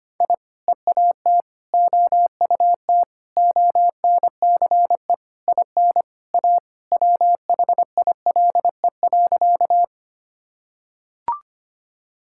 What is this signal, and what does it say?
Morse code, heard as a keyed tone, about 25 wpm, 700 Hz. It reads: I EAT OUT ONCE IN A WHILE.
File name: practice / 25wpm / Sentences with 4-Word Phrases - 25wpm